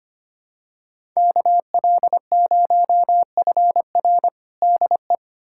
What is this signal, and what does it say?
Morse code, heard as a keyed tone, about 25 wpm, 700 Hz. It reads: KL0FR DE
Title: USA Callsign DE Callsign - 5 Characters - 25wpm